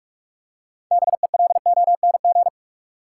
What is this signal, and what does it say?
Morse code, heard as a keyed tone, about 46 wpm, 700 Hz. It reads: BELONG
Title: Six Letter Words - 45wpm